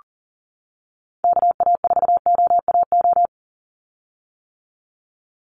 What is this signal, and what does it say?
Morse code, heard as a keyed tone, about 40 wpm, 700 Hz. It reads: KA4OAO
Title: USA Call Signs 40wpm